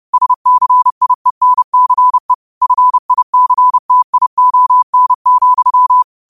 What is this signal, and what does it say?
Morse code, heard as a keyed tone, about 30 wpm, 1000 Hz. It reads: SCIENCE FICTION,
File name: qotdfiles / 30wpm